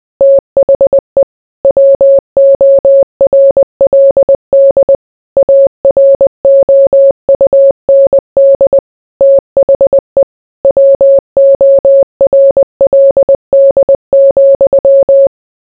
Morse code, heard as a keyed tone, about 20 wpm, 550 Hz. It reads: THE WORLD AROUND THE WORLD,